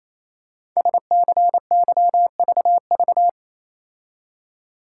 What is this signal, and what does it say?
Morse code, heard as a keyed tone, about 29 wpm, 700 Hz. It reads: SCYVV
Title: Code Groups - 5 Letters - 28wpm